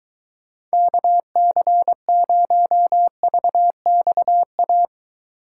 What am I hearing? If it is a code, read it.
Morse code, heard as a keyed tone, about 23 wpm, 700 Hz. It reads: KC0VXA